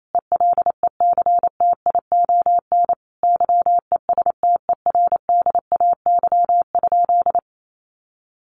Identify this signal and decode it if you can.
Morse code, heard as a keyed tone, about 28 wpm, 700 Hz. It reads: ELECTION YESTERDAY?
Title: A Sound Of Thunder 28wpm